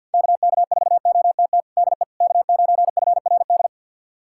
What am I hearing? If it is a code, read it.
Morse code, heard as a keyed tone, about 51 wpm, 700 Hz. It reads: KK4OTT DE K9FRD
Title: USA Callsign DE Callsign 50wpm